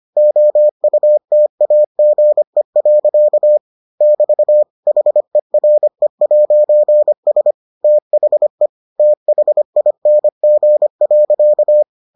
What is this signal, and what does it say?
Morse code, heard as a keyed tone, about 25 wpm, 600 Hz. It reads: OUTAGE. <BT> HERE'S THE THING.